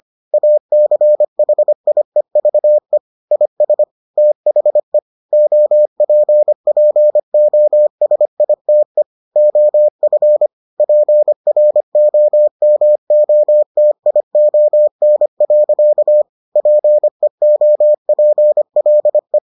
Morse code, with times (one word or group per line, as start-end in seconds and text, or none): 0.3-3.0 ACHIEVE
3.3-3.8 IS
4.2-5.0 THE
5.3-9.0 OPPOSITE
9.4-10.5 OF
10.8-16.2 PROMOTION.
16.6-19.4 PEOPLE